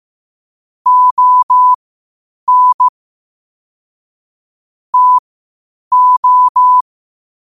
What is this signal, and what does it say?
Morse code, heard as a keyed tone, about 15 wpm, spaced as for 8 wpm, 1000 Hz. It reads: ON TO
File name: qotdfiles / 8wpm